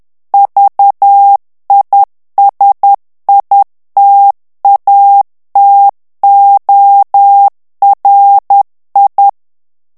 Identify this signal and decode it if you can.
Morse code, heard as a keyed tone, about 10 wpm, 800 Hz. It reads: VISITATORI
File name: morse mp3 / ITALIEN texte clair 1 vitesse 10 mots minute